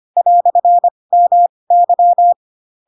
Morse code, heard as a keyed tone, about 25 wpm, spaced as for 20 wpm, 700 Hz. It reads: "MY
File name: Pinocchio / Pinocchio ch7 15 25 ik1hge